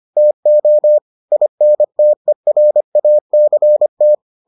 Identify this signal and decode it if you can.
Morse code, heard as a keyed tone, about 25 wpm, 600 Hz. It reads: TO INTERACT